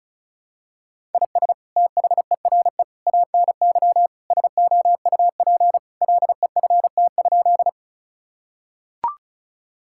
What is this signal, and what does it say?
Morse code, heard as a keyed tone, about 34 wpm, 700 Hz. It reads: IS THERE ANY SOUP LEFT?
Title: Sentences from Top 1000 Words - Review Part 6 - 35wpm